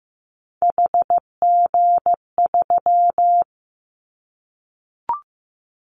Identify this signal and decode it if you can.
Morse code, heard as a keyed tone, about 15 wpm, 700 Hz. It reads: HG3
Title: Callsign Prefixes for Top 50 DXCC - 15wpm